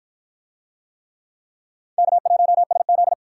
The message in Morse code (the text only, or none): K1SZ